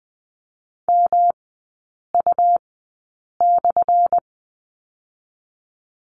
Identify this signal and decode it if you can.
Morse code, heard as a keyed tone, about 20 wpm, spaced as for 8 wpm, 700 Hz. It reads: MU/
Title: 178 Lesson 29e - Slash - Three Characters 20wpm